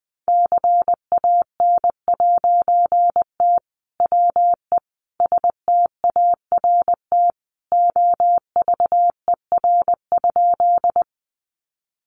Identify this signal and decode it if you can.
Morse code, heard as a keyed tone, about 20 wpm, 700 Hz. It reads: CAN'T WE START OVER?
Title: A Sound Of Thunder 20wpm